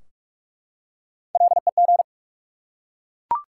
Morse code, heard as a keyed tone, about 44 wpm, 700 Hz. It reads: LEG
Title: Three Letter Words - 45wpm